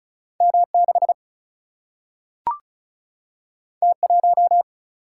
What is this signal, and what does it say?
Morse code, heard as a keyed tone, about 36 wpm, 700 Hz. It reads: M6 T1
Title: Single-Letter Single-Number 35wpm